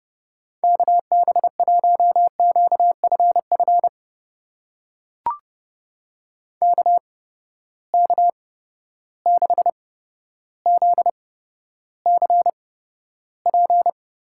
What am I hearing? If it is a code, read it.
Morse code, heard as a keyed tone, about 30 wpm, spaced as for 13 wpm, 700 Hz. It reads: KB1QFF KK6ZCP
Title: USA Call Signs - 6 Characters - 8x Character Spacing - 30wpm